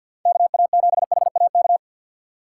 Morse code, heard as a keyed tone, about 52 wpm, 700 Hz. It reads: KA7HAK
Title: USA Call Signs - 6 Characters - 3x Character Spacing - 50wpm